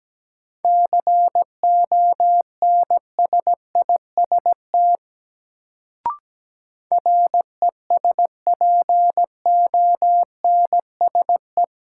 Morse code, heard as a keyed tone, about 17 wpm, 700 Hz. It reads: CONSIST RESPONSE
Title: Sentences from Top 1400 Words - New Word Drill Alt - 17wpm